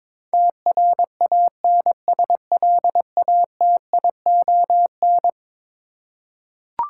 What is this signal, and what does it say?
Morse code, heard as a keyed tone, about 22 wpm, 700 Hz. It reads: TRANSLATION
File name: practice / 22wpm / Eleven Letter Words - 22wpm